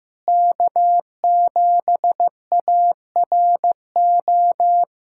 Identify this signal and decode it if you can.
Morse code, heard as a keyed tone, about 16 wpm, 700 Hz. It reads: K7ARO
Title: USA Callsign DE Callsign - 5 Characters - 15wpm